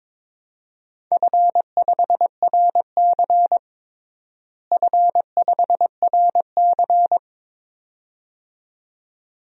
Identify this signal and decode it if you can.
Morse code, heard as a keyed tone, about 22 wpm, 700 Hz. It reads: F5RC F5RC
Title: DXCC-100 Call Signs - 4 Characters - Repeated 2x - 22wpm